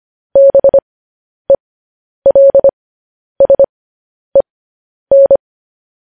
Morse code, heard as a keyed tone, about 25 wpm, spaced as for 10 wpm, 550 Hz. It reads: BELSEN
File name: podcast / News headlines-1x-25x10